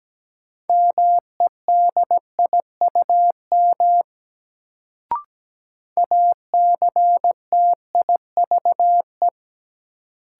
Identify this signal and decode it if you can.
Morse code, heard as a keyed tone, about 17 wpm, 700 Hz. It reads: MEDIUM ACTIVE
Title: Sentences from Top 900 Words - New Word Drill - 17wpm